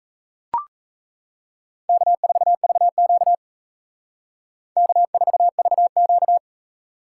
Morse code, heard as a keyed tone, about 38 wpm, 700 Hz. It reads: K4VQ K4VQ